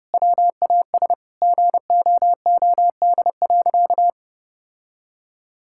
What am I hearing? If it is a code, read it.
Morse code, heard as a keyed tone, about 30 wpm, 700 Hz. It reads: WAS GOOD.
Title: Sentences from Top 1600 Words 30wpm